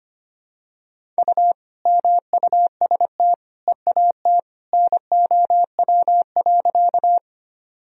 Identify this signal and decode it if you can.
Morse code, heard as a keyed tone, about 25 wpm, 700 Hz. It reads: U MUST EAT NOW.
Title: Sentences from Top 300 Words and Abbreviations 25wpm